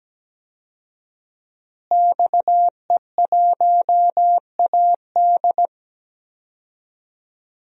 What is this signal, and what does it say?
Morse code, heard as a keyed tone, about 17 wpm, 700 Hz. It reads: XE1AD
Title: DXCC-100 Call Signs - 5 Characters - Repeated 2x - 17wpm